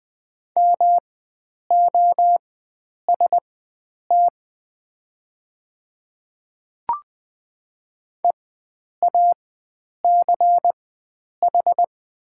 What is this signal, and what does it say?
Morse code, heard as a keyed tone, about 20 wpm, spaced as for 9 wpm, 690 Hz. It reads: MOST EACH
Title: 4x Character Spacing - Top 100 Words - 20wpm